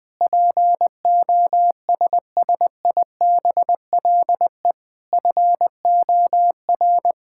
Morse code, heard as a keyed tone, about 20 wpm, 700 Hz. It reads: POSSIBLE FOR
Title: Sentences with 7-Word Phrases - Spoken Beginning - 20wpm